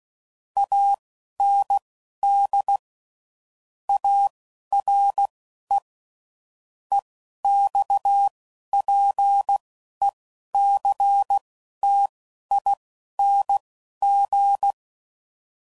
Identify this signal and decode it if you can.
Morse code, heard as a keyed tone, about 16 wpm, spaced as for 11 wpm, 800 Hz. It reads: AND ARE EXPECTING
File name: CW Code Groups @10WPM / QSOs / QSO WA7QOX